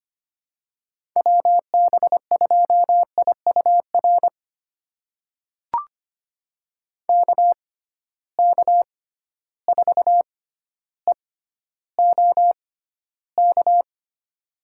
Morse code, heard as a keyed tone, about 25 wpm, spaced as for 13 wpm, 700 Hz. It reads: WB2IUR KK4EOK
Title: USA Call Signs - 6 Characters - 6x Character Spacing - 25wpm